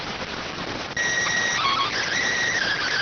SSTV (slow-scan television), PD 290.